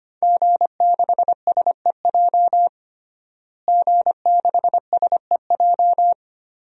Morse code, heard as a keyed tone, about 25 wpm, 700 Hz. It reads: G6SEJ G6SEJ
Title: DXCC-100 Call Signs - 5 Characters - Repeated 2x - 25wpm